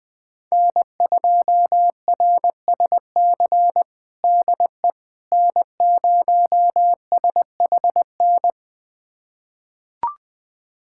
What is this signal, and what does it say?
Morse code, heard as a keyed tone, about 20 wpm, 700 Hz. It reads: N2RSC DE N0SHN
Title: USA Callsign DE Callsign - 5 Characters - 20wpm